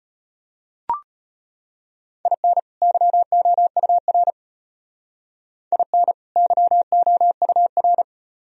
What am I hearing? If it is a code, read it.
Morse code, heard as a keyed tone, about 32 wpm, 700 Hz. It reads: IN YOUR IN YOUR